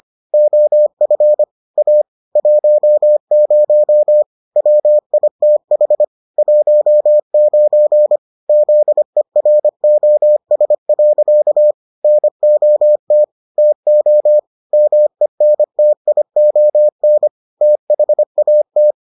Morse code, heard as a keyed tone, about 25 wpm, 600 Hz. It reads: OF A 10 WITH 19 ZEROS. NOT TO MENTION THAT